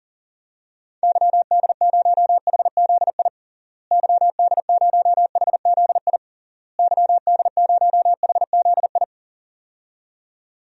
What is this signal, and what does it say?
Morse code, heard as a keyed tone, about 37 wpm, 700 Hz. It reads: YD0HZI YD0HZI YD0HZI
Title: DXCC-100 Call Signs - 6 Characters - Repeated 3x - 40wpm